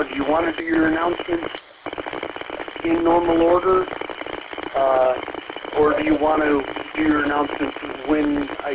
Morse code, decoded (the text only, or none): none